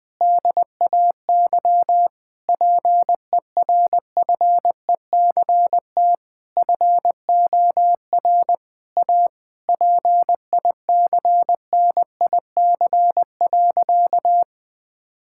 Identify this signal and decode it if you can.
Morse code, heard as a keyed tone, about 20 wpm, 700 Hz. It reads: DAY PERFECT FOR A PICNIC.
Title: Difficult Sentences with Hints - 20wpm